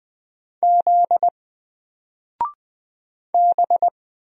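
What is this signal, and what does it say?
Morse code, heard as a keyed tone, about 20 wpm, 700 Hz. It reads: Z B
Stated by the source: Single letter and number Morse drill at 20 wpm, from N4LCD.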